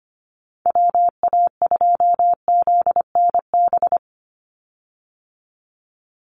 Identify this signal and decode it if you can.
Morse code, heard as a keyed tone, about 25 wpm, 700 Hz. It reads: WA2ZNB
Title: USA Call Signs 25wpm